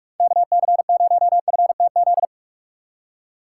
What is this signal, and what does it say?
Morse code, heard as a keyed tone, about 43 wpm, 700 Hz. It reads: KC0FTZ
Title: USA Call Signs - 6 Characters - 2x Character Spacing - 45wpm